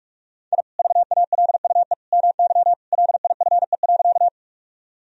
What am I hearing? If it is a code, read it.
Morse code, heard as a keyed tone, about 46 wpm, 700 Hz. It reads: I VALUE MY LIFE.
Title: Sentences from Top 1000 Words - Review Part 3 - 45wpm